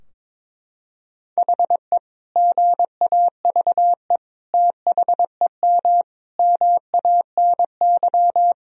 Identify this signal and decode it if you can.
Morse code, heard as a keyed tone, about 22 wpm, 700 Hz. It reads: HE GAVE THEM MANY
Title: Sentences from Top 1000 Words - Review Part 3 - 22wpm